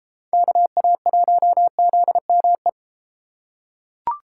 Morse code, heard as a keyed tone, about 31 wpm, 700 Hz. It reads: KA1ZME